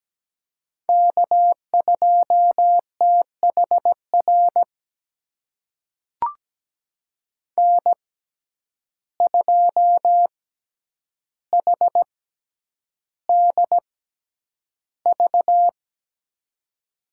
Morse code, heard as a keyed tone, about 17 wpm, spaced as for 9 wpm, 700 Hz. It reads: K2THR N2HDV
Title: USA Call Signs - 5 Characters - 6x Character Spacing - 17wpm